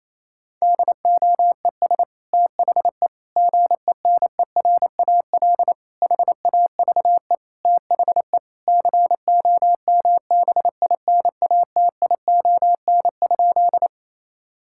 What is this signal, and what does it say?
Morse code, heard as a keyed tone, about 27 wpm, 700 Hz. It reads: DOES THE GENERAL HAVE THE COMBINATION?
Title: Sentences from Top 1000 Words - Review Part 6 - 28wpm